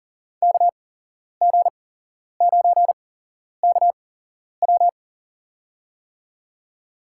Morse code, heard as a keyed tone, about 40 wpm, spaced as for 11 wpm, 700 Hz. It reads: KG9KW
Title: USA Call Signs - 5 Characters - 8x Character Spacing - 40wpm